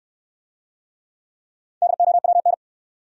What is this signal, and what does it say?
Morse code, heard as a keyed tone, about 66 wpm, 700 Hz. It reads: DOWN